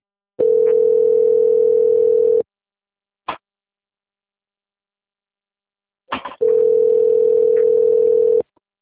Ringback tone (North America), 440 + 480 Hz.
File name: out- - - - - .3927